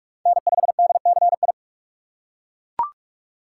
Morse code, heard as a keyed tone, about 41 wpm, 700 Hz. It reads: N5DCI